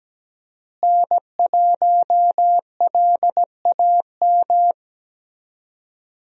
Morse code, heard as a keyed tone, about 17 wpm, 700 Hz. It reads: N1LAM
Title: USA Call Signs - 5 Characters - Repeated 3x - 17wpm